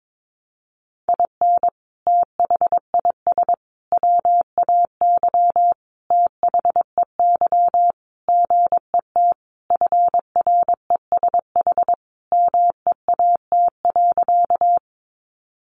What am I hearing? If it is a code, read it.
Morse code, heard as a keyed tone, about 22 wpm, 700 Hz. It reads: IN THIS WAY THEY GET FRESH MEAT.
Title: Sentences from Top 1200 Words 22wpm